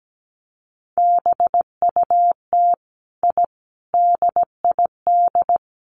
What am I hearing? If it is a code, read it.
Morse code, heard as a keyed tone, about 17 wpm, 700 Hz. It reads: BUT I DID
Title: Sets of 5 Words Encore 17wpm